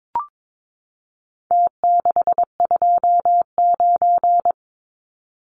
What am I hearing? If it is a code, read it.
Morse code, heard as a keyed tone, about 22 wpm, 700 Hz. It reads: T629
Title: Numbers - Contest Style - 4-Digit 22wpm